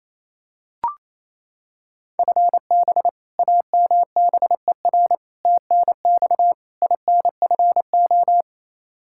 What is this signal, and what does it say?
Morse code, heard as a keyed tone, about 28 wpm, 700 Hz. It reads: FB AMBER TNX INFO
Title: Practice QSO - Exchange 3 - More Info - 28wpm